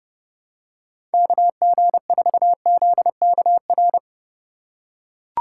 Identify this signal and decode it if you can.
Morse code, heard as a keyed tone, about 30 wpm, 700 Hz. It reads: KG4ZKR
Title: USA Call Signs - 6 Characters - 4x Character Spacing - 30wpm